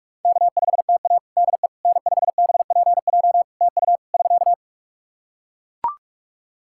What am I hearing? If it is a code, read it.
Morse code, heard as a keyed tone, about 42 wpm, 700 Hz. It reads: K5TA DE N5BPJ TU <SK>
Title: K1USN SST QSO Practice 45wpm